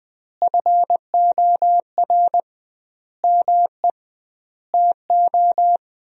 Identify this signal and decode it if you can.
Morse code, heard as a keyed tone, about 20 wpm, 700 Hz. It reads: FOR ME TO
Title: 2x Word-Spacing - Sentences from Top 500 Words - 20wpm